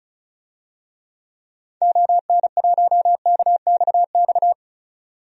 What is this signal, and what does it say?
Morse code, heard as a keyed tone, about 36 wpm, 700 Hz. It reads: ON1KXX
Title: DXCC-100 Call Signs - 6 Characters - 35wpm